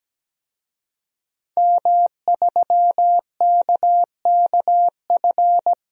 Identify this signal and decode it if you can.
Morse code, heard as a keyed tone, about 17 wpm, 700 Hz. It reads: M3KKF
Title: DXCC-100 Call Signs - 5 Characters - Repeated 3x - 17wpm